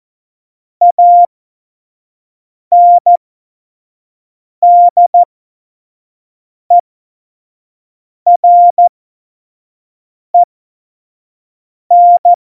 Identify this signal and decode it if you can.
Morse code, heard as a keyed tone, about 14 wpm, spaced as for 5 wpm, 700 Hz. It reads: ANDEREN
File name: Morsekurs / Witze / Witz 18. Dezember 2018, 25 F70